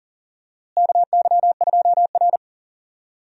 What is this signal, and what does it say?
Morse code, heard as a keyed tone, about 40 wpm, 700 Hz. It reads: KY2R
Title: USA Call Signs - 4 Characters - Repeated 3x - 40wpm